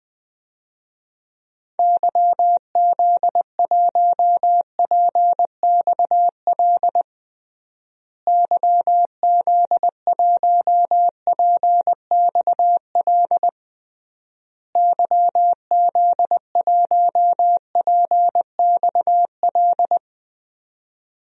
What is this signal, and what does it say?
Morse code, heard as a keyed tone, about 20 wpm, 700 Hz. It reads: YZ1PXL YZ1PXL YZ1PXL
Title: DXCC-100 Call Signs - 6 Characters - Repeated 3x - 20wpm